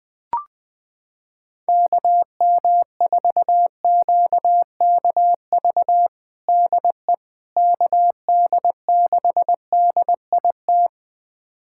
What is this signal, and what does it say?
Morse code, heard as a keyed tone, about 20 wpm, 700 Hz. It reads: KM4QKV DE KD6DIT
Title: USA Callsign DE Callsign - 6 Characters - 20wpm